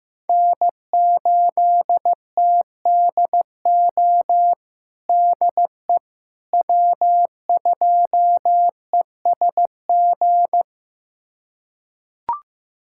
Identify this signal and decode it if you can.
Morse code, heard as a keyed tone, about 15 wpm, 700 Hz. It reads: N8TDO DE W2ESG
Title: USA Callsign DE Callsign - 5 Characters - 15wpm